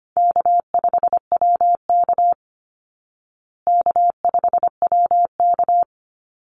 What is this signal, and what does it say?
Morse code, heard as a keyed tone, about 25 wpm, 700 Hz. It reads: K5WK K5WK
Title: Easy Call Signs Repeated 3x - 25wpm